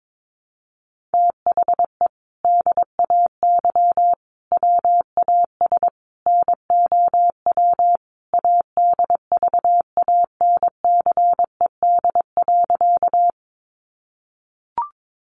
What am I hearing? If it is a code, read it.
Morse code, heard as a keyed tone, about 22 wpm, 700 Hz. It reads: THE DAY WAS NOW ADVANCED.
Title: Sentences from Top 1200 Words 22wpm